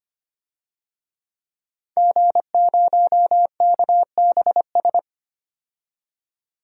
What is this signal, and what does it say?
Morse code, heard as a keyed tone, about 25 wpm, 700 Hz. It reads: G0KBS
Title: DXCC-100 Call Signs - 5 Characters - 25wpm